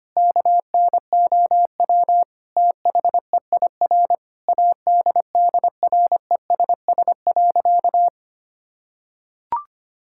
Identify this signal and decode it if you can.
Morse code, heard as a keyed tone, about 25 wpm, 700 Hz. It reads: KNOW THEIR ADDRESS.
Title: Sentences from Top 1000 Words - Review Part 6 - 25wpm